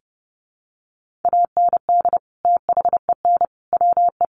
Morse code, heard as a keyed tone, about 30 wpm, 700 Hz. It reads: AND THEN WE
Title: Sentences from Top 300 Words 30wpm